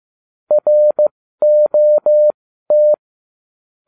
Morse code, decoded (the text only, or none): ROT